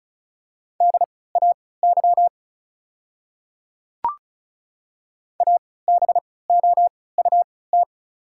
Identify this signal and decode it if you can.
Morse code, heard as a keyed tone, about 36 wpm, spaced as for 20 wpm, 690 Hz. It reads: DAY ABOUT
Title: 3x Character Spacing - Top 100 Words - 35wpm